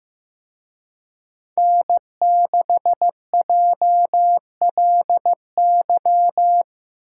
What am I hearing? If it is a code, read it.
Morse code, heard as a keyed tone, about 18 wpm, 700 Hz. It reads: N6JLY
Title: Speed Racing - USA Call Signs 15wpm